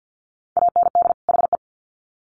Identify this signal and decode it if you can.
Morse code, heard as a keyed tone, about 50 wpm, 700 Hz. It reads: AND HE
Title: Sets of 2 Words 50wpm